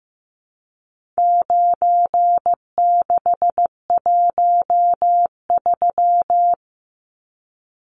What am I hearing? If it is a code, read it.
Morse code, heard as a keyed tone, about 15 wpm, 700 Hz. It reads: 9613
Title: Numbers - 4-Digit 15wpm